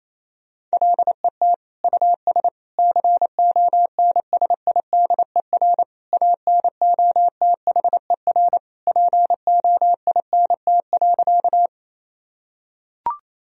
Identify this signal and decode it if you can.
Morse code, heard as a keyed tone, about 28 wpm, 700 Hz. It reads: LET US CONSIDER ANOTHER POINT.
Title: Sentences from Top 1000 Words - Review Part 4 - 28wpm